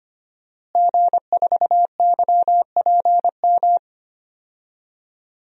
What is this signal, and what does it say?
Morse code, heard as a keyed tone, about 25 wpm, 700 Hz. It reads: G4YPM
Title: DXCC-100 Call Signs - 5 Characters - Repeated 2x - 25wpm